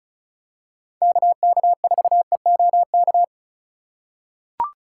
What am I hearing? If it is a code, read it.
Morse code, heard as a keyed tone, about 34 wpm, 700 Hz. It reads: KK4EOK
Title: USA Call Signs - 6 Characters - 4x Character Spacing - 35wpm